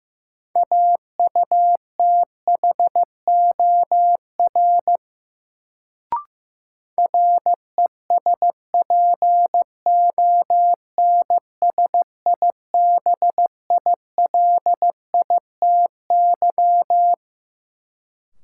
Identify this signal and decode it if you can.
Morse code, heard as a keyed tone, about 15 wpm, 700 Hz. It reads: AUTHOR RESPONSIBILITY
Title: Sentences from Top 1200 Words - New Word Drill - 15wpm